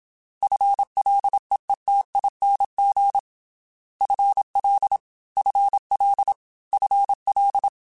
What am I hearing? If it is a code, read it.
Morse code, heard as a keyed tone, about 27 wpm, 800 Hz. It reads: FLEETING FL FL FL
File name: FL / FL mp3s / FL Prefixes 3x2 Twice @25